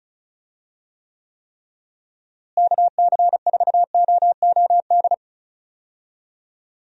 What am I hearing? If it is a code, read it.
Morse code, heard as a keyed tone, about 36 wpm, 700 Hz. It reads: KC4OOD
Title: USA Call Signs - 6 Characters - Repeated 3x - 35wpm